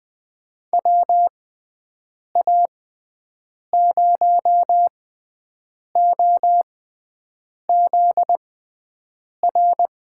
Morse code, heard as a keyed tone, about 20 wpm, spaced as for 7 wpm, 700 Hz. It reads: WA0OZR
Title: USA Call Signs - 6 Characters - 6x Character Spacing - 20wpm